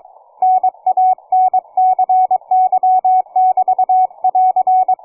Morse code, heard as a keyed tone, about 23 wpm, 740 Hz, with radio noise. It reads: NANCY<BT><AR>
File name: cw forum / 9m2ax 160m1829 22h14